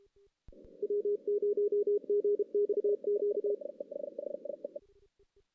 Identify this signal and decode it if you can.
Morse code, heard as a keyed tone, about 31 wpm, 410 Hz. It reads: W0GXQ
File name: 2016 JAN CW NAQP / 2016 Jan naqp k5ot - 0-5